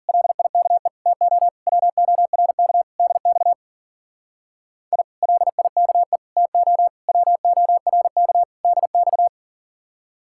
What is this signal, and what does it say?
Morse code, heard as a keyed tone, about 48 wpm, 700 Hz. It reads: LIKE TO WORK DX I LIKE TO WORK DX
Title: Speed-Racing Common QSO Phrases - 40wpm